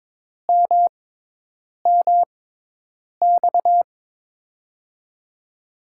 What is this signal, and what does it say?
Morse code, heard as a keyed tone, about 22 wpm, spaced as for 8 wpm, 700 Hz. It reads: MMX
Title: USA Call Signs - 6 Characters - 6x Character Spacing - 22wpm